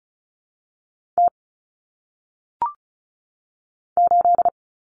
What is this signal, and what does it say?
Morse code, heard as a keyed tone, about 36 wpm, 700 Hz. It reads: T 8
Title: Common Cut Numbers - 1-Digit 35wpm